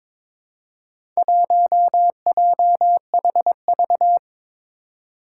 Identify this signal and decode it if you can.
Morse code, heard as a keyed tone, about 21 wpm, 700 Hz. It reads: 1JHV
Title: Four Challenging Letter-Number Combinations 22wpm